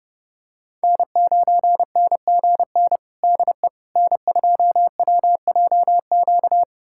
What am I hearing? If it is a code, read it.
Morse code, heard as a keyed tone, about 30 wpm, 700 Hz. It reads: N9NGN DE N2WJQ
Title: USA Callsign DE Callsign 30wpm